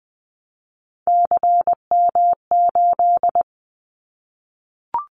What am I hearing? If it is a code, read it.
Morse code, heard as a keyed tone, about 20 wpm, 700 Hz. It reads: CM8
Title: Callsign Prefixes for Top 100 DXCC - 20wpm